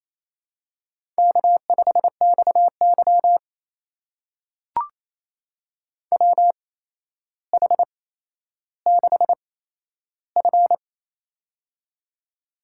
Morse code, heard as a keyed tone, about 27 wpm, spaced as for 12 wpm, 700 Hz. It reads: K5XY WH6F
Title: USA Call Signs - 4 Characters - 8x Character Spacing - 28wpm